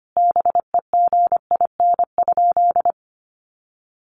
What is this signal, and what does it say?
Morse code, heard as a keyed tone, about 25 wpm, 700 Hz. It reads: BEGIN?
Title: Sentences from Top 300 Words 25wpm